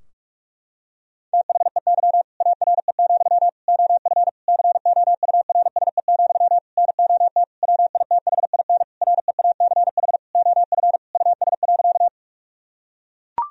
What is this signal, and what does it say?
Morse code, heard as a keyed tone, about 41 wpm, 700 Hz. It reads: THEY ARE, OF COURSE, NOT WITHIN REACH OF US.
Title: Sentences from Top 1000 Words - Review Part 4 - 45wpm